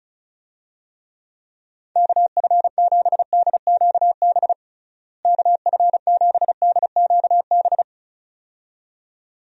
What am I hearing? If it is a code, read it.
Morse code, heard as a keyed tone, about 36 wpm, 700 Hz. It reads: KF7DQB KF7DQB
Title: USA Call Signs - 6 Characters - Repeated 2x - 35wpm